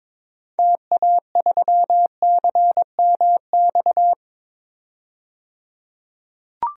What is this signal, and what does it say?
Morse code, heard as a keyed tone, about 22 wpm, 700 Hz. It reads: TA3CMX
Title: DXCC-100 Call Signs - 6 Characters - 22wpm